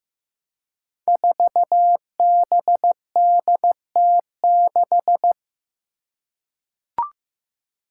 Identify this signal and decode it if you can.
Morse code, heard as a keyed tone, about 15 wpm, 700 Hz. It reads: 4BDT6
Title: Five Challenging Letter-Number Combinations 15wpm